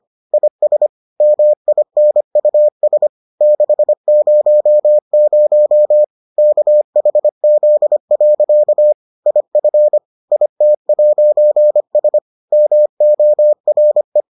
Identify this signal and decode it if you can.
Morse code, heard as a keyed tone, about 25 wpm, 600 Hz. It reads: IS MINUS 600 KHZ. IF IT'S MORE